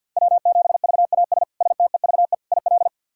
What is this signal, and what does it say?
Morse code, heard as a keyed tone, about 51 wpm, 700 Hz. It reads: W7VAS STEVE IL
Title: K1USN SST QSO Practice - 50wpm